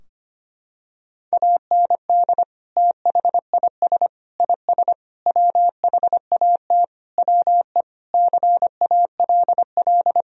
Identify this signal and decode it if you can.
Morse code, heard as a keyed tone, about 25 wpm, 700 Hz. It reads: AND THIS IS WHAT WE CALL
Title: Sentences from Top 1000 Words - Review Part 3 - 25wpm